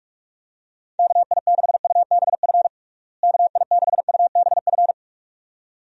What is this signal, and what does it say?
Morse code, heard as a keyed tone, about 46 wpm, 700 Hz. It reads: KI6UBF KI6UBF
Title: USA Call Signs - 6 Characters - Repeated 2x - 45wpm